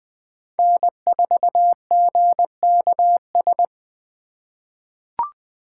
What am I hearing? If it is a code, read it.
Morse code, heard as a keyed tone, about 20 wpm, 700 Hz. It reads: N4GKS
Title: USA Call Signs - 5 Characters - 2x Character Spacing - 20wpm